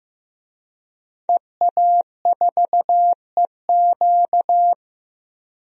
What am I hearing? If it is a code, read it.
Morse code, heard as a keyed tone, about 15 wpm, 700 Hz. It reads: EA4EQ
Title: DXCC-100 Call Signs - 5 Characters - Repeated 3x - 15wpm